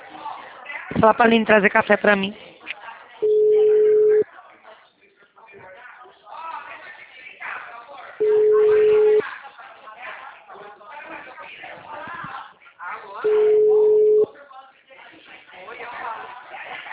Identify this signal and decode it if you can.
Ringback tone (Europe, 425 Hz), with speech.